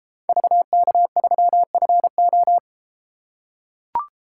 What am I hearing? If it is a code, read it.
Morse code, heard as a keyed tone, about 31 wpm, 700 Hz. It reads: VK3FO